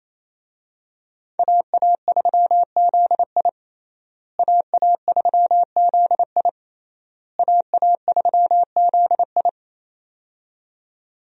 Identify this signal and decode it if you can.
Morse code, heard as a keyed tone, about 28 wpm, 700 Hz. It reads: AA3ZI AA3ZI AA3ZI
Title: USA Call Signs - 5 Characters - Repeated 3x - 28wpm